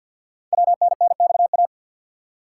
Morse code, heard as a keyed tone, about 53 wpm, 700 Hz. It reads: WNNXA
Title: Code Groups - 5 Letters - 50wpm